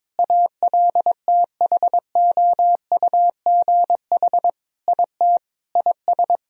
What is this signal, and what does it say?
Morse code, heard as a keyed tone, about 22 wpm, 700 Hz. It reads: ALTHOUGH IT IS
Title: Sentences from Top 1000 Words - Review Part 3 - 22wpm